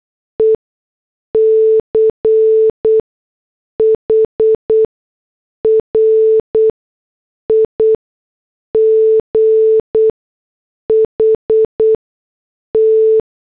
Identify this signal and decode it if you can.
Morse code, heard as a keyed tone, about 8 wpm, spaced as for 6 wpm, 440 Hz. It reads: ECH RIGHT